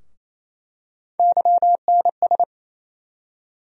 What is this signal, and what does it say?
Morse code, heard as a keyed tone, about 28 wpm, 700 Hz. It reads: YNS
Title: USA Call Sign Postfixes - 3 Letters 28wpm